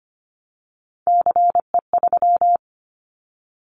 Morse code, heard as a keyed tone, about 25 wpm, 700 Hz. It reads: CE3
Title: Callsign Prefixes for Top 100 DXCC - 25wpm